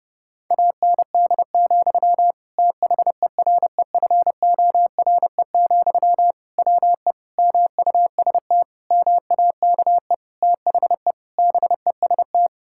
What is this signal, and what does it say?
Morse code, heard as a keyed tone, about 30 wpm, 700 Hz. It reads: AND, THEREFORE, WE MUST MAKE THE BEST
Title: Sentences from Top 1000 Words - Review Part 5 - 30wpm